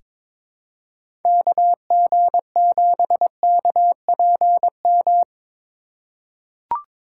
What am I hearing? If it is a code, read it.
Morse code, heard as a keyed tone, about 23 wpm, 700 Hz. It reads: KG7KPM
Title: USA Call Signs - 6 Characters - 4x Character Spacing - 22wpm